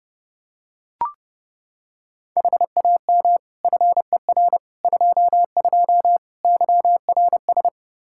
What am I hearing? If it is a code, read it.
Morse code, heard as a keyed tone, about 30 wpm, 700 Hz. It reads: HAM FER 22 YRS
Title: K1USN SST QSO Practice 30wpm